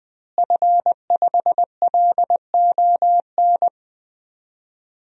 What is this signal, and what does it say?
Morse code, heard as a keyed tone, about 20 wpm, 700 Hz. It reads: F5LON
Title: DXCC-100 Call Signs - 5 Characters - Repeated 2x - 20wpm